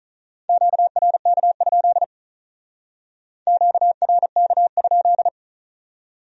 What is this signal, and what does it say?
Morse code, heard as a keyed tone, about 43 wpm, 700 Hz. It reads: QRK? QRK?
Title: Speed-Racing Q-Codes - 35wpm